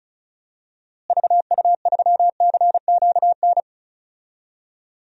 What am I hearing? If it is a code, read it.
Morse code, heard as a keyed tone, about 36 wpm, 700 Hz. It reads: VU3CQN
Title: DXCC-100 Call Signs - 6 Characters - Repeated 3x - 35wpm